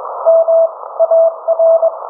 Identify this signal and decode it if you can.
Morse code, heard as a keyed tone, about 21 wpm, 660 Hz. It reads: M AR